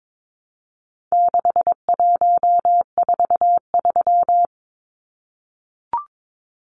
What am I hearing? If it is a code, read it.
Morse code, heard as a keyed tone, about 22 wpm, 700 Hz. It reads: 6143
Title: Numbers - 4-Digit 22wpm